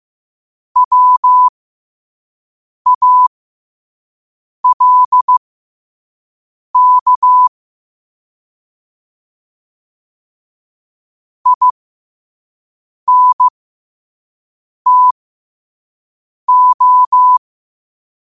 Morse code, heard as a keyed tone, about 15 wpm, spaced as for 5 wpm, 1000 Hz. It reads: WALK INTO
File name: 5wpm